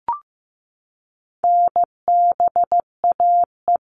Morse code, heard as a keyed tone, about 15 wpm, 700 Hz. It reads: NBAE
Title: All Cut Numbers - 4-Digit 15wpm